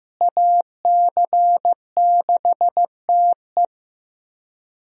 Morse code, heard as a keyed tone, about 15 wpm, 700 Hz. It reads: AC6TE
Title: USA Call Signs - 5 Characters - Repeated 2x - 15wpm